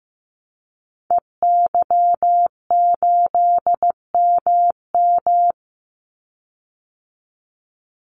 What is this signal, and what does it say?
Morse code, heard as a keyed tone, about 15 wpm, 700 Hz. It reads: EY8MM
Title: Call Signs Repeated 3x - 15wpm